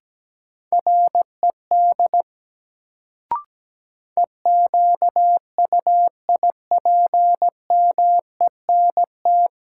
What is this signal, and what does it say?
Morse code, heard as a keyed tone, about 17 wpm, 700 Hz. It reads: RED EQUIPMENT
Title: Sentences from Top 700 Words - New Word Drill - 17wpm